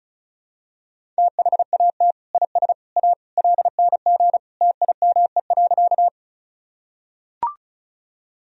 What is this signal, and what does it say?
Morse code, heard as a keyed tone, about 36 wpm, 700 Hz. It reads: THAT IS A LNG TIME.
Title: Sentences from Top 100 Words and Abbreviations 35wpm